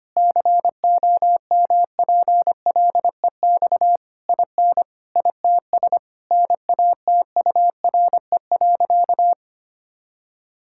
Morse code, heard as a keyed tone, about 25 wpm, 700 Hz. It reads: COMPLEX IN ITS NATURE.
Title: Sentences from Top 1000 Words - Review Part 6 - 25wpm